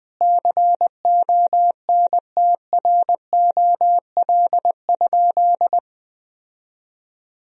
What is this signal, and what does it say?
Morse code, heard as a keyed tone, about 21 wpm, 700 Hz. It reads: CONTROL?
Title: Sentences from Top 1000 Words - Review Part 3 - 20wpm